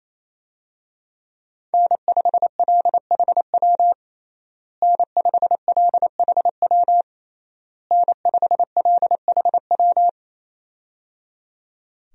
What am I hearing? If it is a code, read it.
Morse code, heard as a keyed tone, about 28 wpm, 700 Hz. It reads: N5LHW N5LHW N5LHW
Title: USA Call Signs - 5 Characters - Repeated 3x - 28wpm